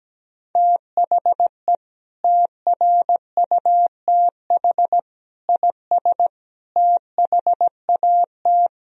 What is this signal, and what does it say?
Morse code, heard as a keyed tone, about 17 wpm, 700 Hz. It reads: THE TRUTH IS THAT
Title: Sentences from Top 1000 Words - Review Part 1 - 17wpm